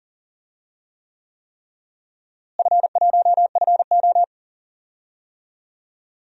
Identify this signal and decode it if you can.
Morse code, heard as a keyed tone, about 37 wpm, 700 Hz. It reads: F1FO
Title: DXCC-100 Call Signs - 4 Characters - Repeated 3x - 40wpm